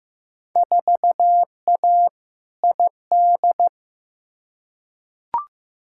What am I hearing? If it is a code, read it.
Morse code, heard as a keyed tone, about 15 wpm, 700 Hz. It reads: 4A ID
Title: Field Day Exchanges Encore - 15wpm